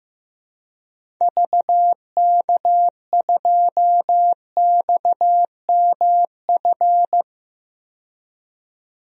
Morse code, heard as a keyed tone, about 16 wpm, 700 Hz. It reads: VK2XMF
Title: DXCC-100 Call Signs - 6 Characters - 15wpm